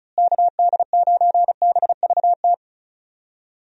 Morse code, heard as a keyed tone, about 34 wpm, 700 Hz. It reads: KD9BVT
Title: USA Call Signs - 6 Characters - Repeated 2x - 35wpm